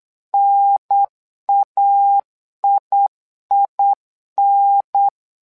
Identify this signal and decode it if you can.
Morse code, heard as a keyed tone, about 8 wpm, 800 Hz. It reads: NAIIN
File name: LEZIONI / 2 UHDN / 2 UHDN 8wpm2232